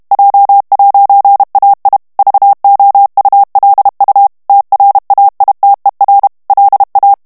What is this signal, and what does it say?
Morse code, heard as a keyed tone, about 32 wpm, 800 Hz. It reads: J'AI VOULU TRAITER LA
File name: FRANCAIS TEXTE CLAIR 2 VITESSE 30 MOTS MINUTES